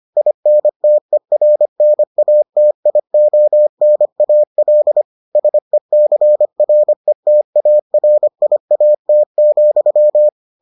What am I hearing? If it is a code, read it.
Morse code, heard as a keyed tone, about 25 wpm, 600 Hz. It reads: INTERNATIONAL SECRETARIAT,